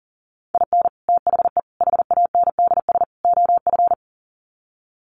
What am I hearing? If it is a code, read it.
Morse code, heard as a keyed tone, about 40 wpm, 700 Hz. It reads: IN THE HANDS OF
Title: Sets of 4 Words 40wpm